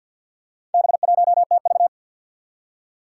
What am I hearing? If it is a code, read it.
Morse code, heard as a keyed tone, about 52 wpm, 700 Hz. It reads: B1TV
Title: Four Challenging Letter-Number Combinations 50wpm